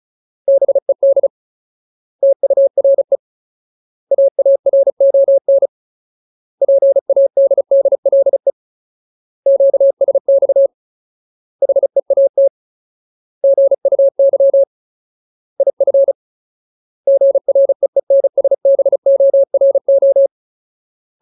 Morse code, heard as a keyed tone, about 34 wpm, 550 Hz. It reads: BED TURE AARON PADDLE QSX HEAT GUY IF GREENSBORO
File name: Ultimate Word List / 35WPM Ultimate